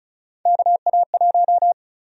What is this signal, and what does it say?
Morse code, heard as a keyed tone, about 36 wpm, 700 Hz. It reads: KA1
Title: USA Call Sign Prefixes - 35wpm